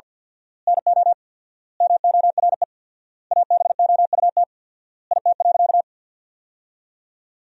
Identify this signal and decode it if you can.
Morse code, heard as a keyed tone, about 51 wpm, 700 Hz. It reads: NO MORE ABOUT IT.
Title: 4x Word-Spacing - Sentences from Top 200 Words - 50wpm